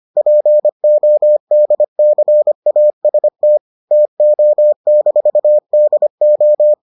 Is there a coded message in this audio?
Yes: Morse code from 0.2 to 6.7 s, heard as a keyed tone, about 25 wpm, 600 Hz. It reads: PODCAST TO-DO